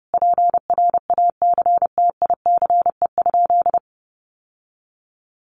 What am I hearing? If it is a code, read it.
Morse code, heard as a keyed tone, about 30 wpm, 700 Hz. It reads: PRACTICE?
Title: Sentences from Top 700 Words 30wpm